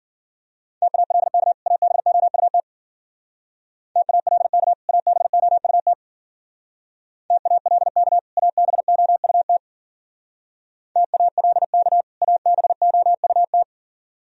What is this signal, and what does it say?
Morse code, heard as a keyed tone, about 50 wpm, 700 Hz. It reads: TALK ABOUT TALK ABOUT TALK ABOUT TALK ABOUT